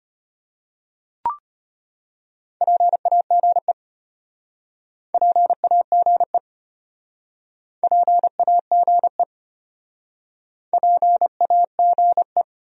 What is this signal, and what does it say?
Morse code, heard as a keyed tone, about 33 wpm, 700 Hz. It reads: PAGE PAGE PAGE PAGE